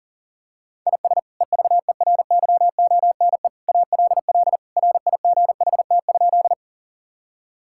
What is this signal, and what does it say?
Morse code, heard as a keyed tone, about 37 wpm, 700 Hz. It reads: IS EVERYONE ALL RIGHT?
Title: Sentences from Top 1000 Words - Review Part 5 - 40wpm